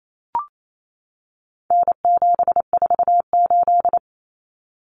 Morse code, heard as a keyed tone, about 28 wpm, 700 Hz. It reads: N748
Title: Common Cut Numbers - 4-Digit 28wpm